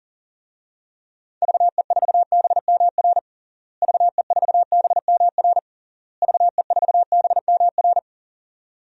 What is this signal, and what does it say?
Morse code, heard as a keyed tone, about 37 wpm, 700 Hz. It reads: VE4BMR VE4BMR VE4BMR
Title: DXCC-100 Call Signs - 6 Characters - Repeated 3x - 40wpm